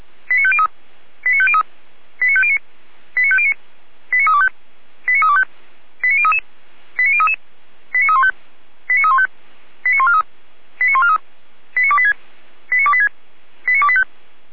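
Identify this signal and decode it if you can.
Five-tone selective call (ZVEI), heard as quick runs of five tones, tones of about 70 ms. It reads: 89503 89503 89509 89509 89325 89325 89033 89033 89315 89315 89143 89143 89286 89286 89285
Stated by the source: ZVEI five-tone selective call received on SDR.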